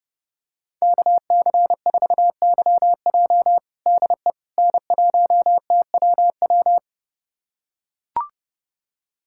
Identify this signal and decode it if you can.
Morse code, heard as a keyed tone, about 30 wpm, 700 Hz. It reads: KC4YJ DE N1TWW
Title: USA Callsign DE Callsign - 5 Characters - 30wpm